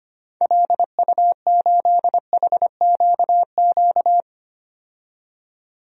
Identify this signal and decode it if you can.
Morse code, heard as a keyed tone, about 25 wpm, 700 Hz. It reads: LU8HQQ
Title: DXCC-100 Call Signs - 6 Characters - 25wpm